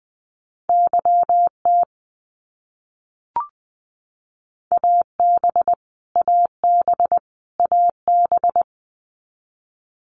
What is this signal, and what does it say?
Morse code, heard as a keyed tone, about 20 wpm, 700 Hz. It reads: YT AB AB AB
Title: Canadian Provinces and Territories - Repeated 3x - 20wpm